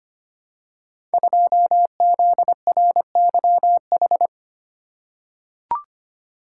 Morse code, heard as a keyed tone, about 25 wpm, 700 Hz. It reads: 2ZRYH
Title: Code Groups - 5 Letter:Numbers - 25wpm